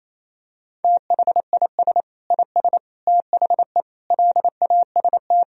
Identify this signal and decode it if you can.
Morse code, heard as a keyed tone, about 28 wpm, 700 Hz. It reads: THIS IS THE LAST
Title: Sentences from Top 1000 Words - Review Part 4 - 28wpm